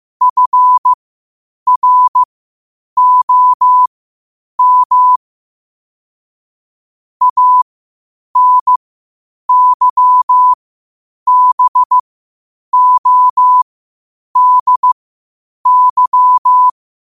Morse code, heard as a keyed tone, about 15 wpm, spaced as for 8 wpm, 1000 Hz. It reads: FROM ANYBODY
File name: qotdfiles / 8wpm